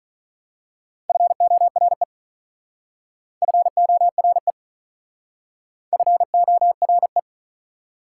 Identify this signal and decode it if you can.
Morse code, heard as a keyed tone, about 40 wpm, 700 Hz. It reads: FORE FORE FORE
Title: Speed-Racing Prefixes and Suffixes 35wpm